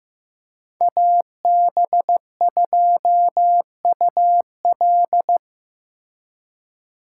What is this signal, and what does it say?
Morse code, heard as a keyed tone, about 16 wpm, 700 Hz. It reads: AB2UL
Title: Speed Racing - USA 2x2 Call Signs 15wpm